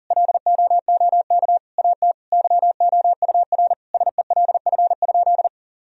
Morse code, heard as a keyed tone, about 40 wpm, 700 Hz. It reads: LOOK AT YOUR SELF?
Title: Sentences from Top 1000 Words - Review Part 1 - 40wpm